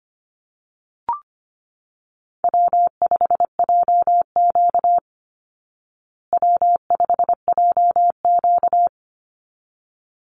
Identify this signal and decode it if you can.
Morse code, heard as a keyed tone, about 25 wpm, 700 Hz. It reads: W5JQ W5JQ
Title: Easy Call Signs Repeated 2x - 25wpm